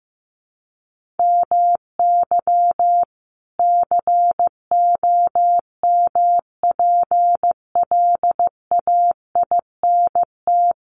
Morse code, heard as a keyed tone, about 15 wpm, 700 Hz. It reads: MY COMPLAINT
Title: Sentences from Top 1400 Words 15wpm